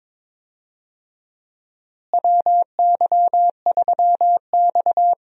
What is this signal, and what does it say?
Morse code, heard as a keyed tone, about 22 wpm, 700 Hz. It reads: WY3X
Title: USA Call Signs - 4 Characters - Repeated 2x - 22wpm